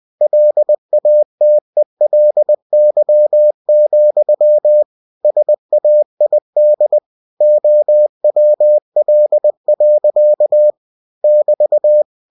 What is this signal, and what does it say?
Morse code, heard as a keyed tone, about 20 wpm, 600 Hz. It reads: LATELY, SAID OWL. <BT>
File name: cw-books / winnie-the-pooh / winnie-20wpm-page-0096